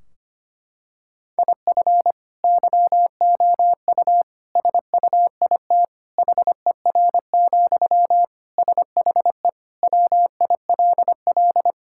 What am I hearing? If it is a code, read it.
Morse code, heard as a keyed tone, about 25 wpm, 700 Hz. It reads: IF YOU SUIT HER, SHE WILL
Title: Sentences from Top 1600 Words 25wpm